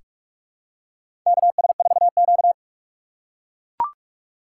Morse code, heard as a keyed tone, about 44 wpm, 700 Hz. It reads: KS4Q
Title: USA Call Signs - 4 Characters - 2x Character Spacing - 45wpm